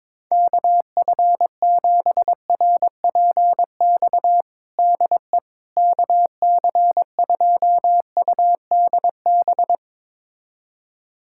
Morse code, heard as a keyed tone, about 22 wpm, 700 Hz. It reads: KF7RPX DE KC2UDB
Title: USA Callsign DE Callsign - 6 Characters - 22wpm